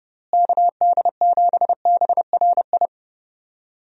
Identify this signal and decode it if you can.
Morse code, heard as a keyed tone, about 30 wpm, 700 Hz. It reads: KD7BRI
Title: USA Call Signs - 6 Characters - Repeated 3x - 30wpm